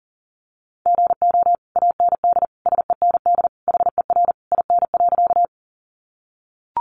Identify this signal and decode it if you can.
Morse code, heard as a keyed tone, about 40 wpm, 700 Hz. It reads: GO AND SEND HER IN.
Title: Sentences from Top 1500 Words 40wpm